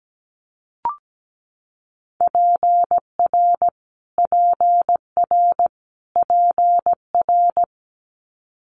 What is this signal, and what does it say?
Morse code, heard as a keyed tone, about 17 wpm, 700 Hz. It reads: PR PR PR
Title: Multi-letter Phonemes Repeated 3x 17wpm